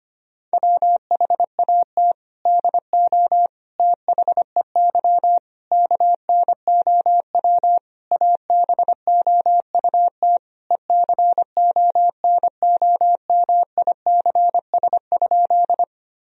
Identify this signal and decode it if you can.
Morse code, heard as a keyed tone, about 25 wpm, 700 Hz. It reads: WHAT DO THEY KNOW ABOUT ECONOMICS?
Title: Sentences from Top 1000 Words - Review Part 3 - 25wpm